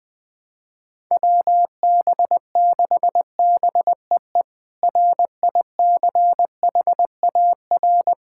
Morse code, heard as a keyed tone, about 20 wpm, 700 Hz. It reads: WB6BEE RICHAR
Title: K1USN SST QSO Practice - 20wpm